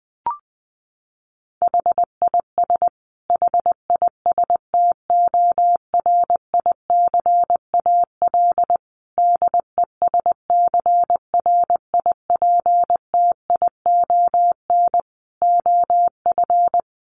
Morse code, heard as a keyed tone, about 20 wpm, 700 Hz. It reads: HIS HISTORICAL DESCRIPTION OF
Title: Sentences from Top 1300 Words 20wpm